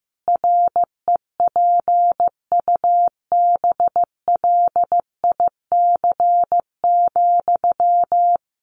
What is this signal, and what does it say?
Morse code, heard as a keyed tone, about 15 wpm, 700 Hz. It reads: REPUBLIC,
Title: Sentences from Top 1500 Words 15wpm